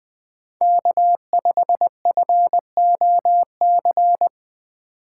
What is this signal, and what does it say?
Morse code, heard as a keyed tone, about 20 wpm, 700 Hz. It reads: K5FOC
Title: USA Call Signs - 5 Characters - 4x Character Spacing - 20wpm